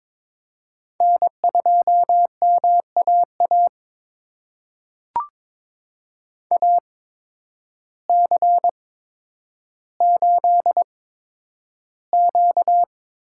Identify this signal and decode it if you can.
Morse code, heard as a keyed tone, about 22 wpm, spaced as for 10 wpm, 700 Hz. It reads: N2MAA AC8Q
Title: USA Call Signs - 8x Character Spacing - 22wpm